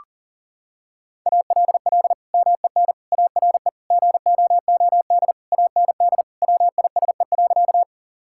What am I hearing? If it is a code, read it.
Morse code, heard as a keyed tone, about 40 wpm, 700 Hz. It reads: ALL MEN ARE GOOD AND WISE.
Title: Sentences from Top 1600 Words 40wpm